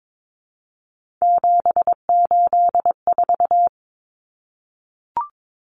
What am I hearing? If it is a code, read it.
Morse code, heard as a keyed tone, about 22 wpm, 700 Hz. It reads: 784
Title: Numbers - 3-Digit 22wpm